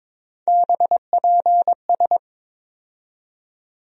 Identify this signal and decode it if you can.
Morse code, heard as a keyed tone, about 22 wpm, 700 Hz. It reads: BPS
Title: USA Call Sign Postfixes - 3 Letters 22wpm